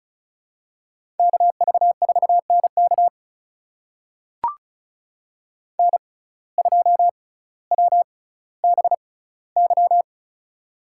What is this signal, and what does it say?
Morse code, heard as a keyed tone, about 36 wpm, spaced as for 18 wpm, 700 Hz. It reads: KV4NK N2WBY
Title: USA Call Signs - 5 Characters - 6x Character Spacing - 35wpm